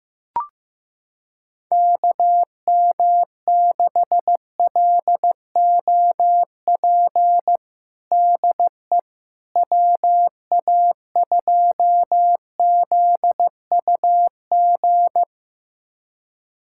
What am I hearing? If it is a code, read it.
Morse code, heard as a keyed tone, about 15 wpm, 700 Hz. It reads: KM6LOP DE WA2ZUG
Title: USA Callsign DE Callsign - 6 Characters - 15wpm